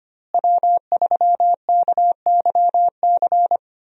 Morse code, heard as a keyed tone, about 25 wpm, 700 Hz. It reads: W3KYC
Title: USA Call Signs - 5 Characters - Repeated 2x - 25wpm